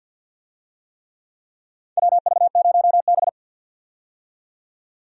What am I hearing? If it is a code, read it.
Morse code, heard as a keyed tone, about 47 wpm, 700 Hz. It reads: WV0B
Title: USA Call Signs - 4 Characters - 50wpm